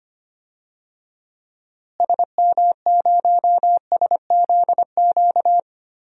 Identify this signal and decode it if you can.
Morse code, heard as a keyed tone, about 25 wpm, 700 Hz. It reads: SM0SZQ
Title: DXCC-100 Call Signs - 6 Characters - Repeated 3x - 25wpm